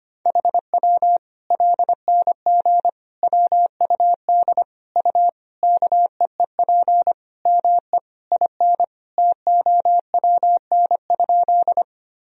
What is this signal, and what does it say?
Morse code, heard as a keyed tone, about 25 wpm, 700 Hz. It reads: HW LNG WUD U KEEP ME IN TOWN?
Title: Sentences from Top 500 Words and Abbreviations 25wpm